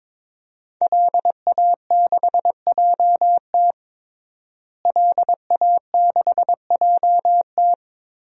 Morse code, heard as a keyed tone, about 22 wpm, 700 Hz. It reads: LA6JT LA6JT
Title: DXCC-100 Call Signs - 5 Characters - Repeated 3x - 22wpm